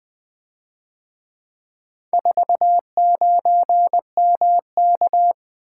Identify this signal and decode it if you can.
Morse code, heard as a keyed tone, about 23 wpm, 700 Hz. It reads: 49MK
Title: Speed-Racing - Four Letter-Number Combinations - 20wpm